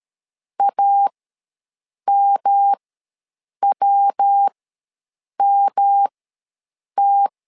Morse code, heard as a keyed tone, about 13 wpm, spaced as for 7 wpm, 800 Hz. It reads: AMWMT
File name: LEZIONI / 3 MWRO / 3 MW ws 12wpm 1816